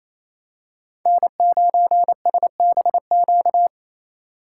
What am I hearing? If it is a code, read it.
Morse code, heard as a keyed tone, about 28 wpm, 700 Hz. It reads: N9SBQ
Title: USA Call Signs - 5 Characters - 2x Character Spacing - 28wpm